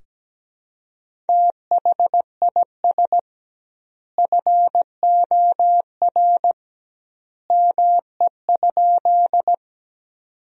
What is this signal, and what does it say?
Morse code, heard as a keyed tone, about 17 wpm, 700 Hz. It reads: THIS FOR ME?
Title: 2x Word-Spacing - Sentences from Top 200 Words - 17wpm